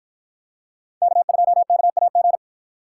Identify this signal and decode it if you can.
Morse code, heard as a keyed tone, about 53 wpm, 700 Hz. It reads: K2KAG